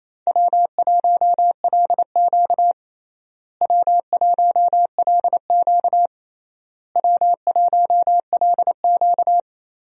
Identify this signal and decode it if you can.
Morse code, heard as a keyed tone, about 28 wpm, 700 Hz. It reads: W1LQ W1LQ W1LQ
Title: USA Call Signs - 4 Characters - Repeated 3x - 28wpm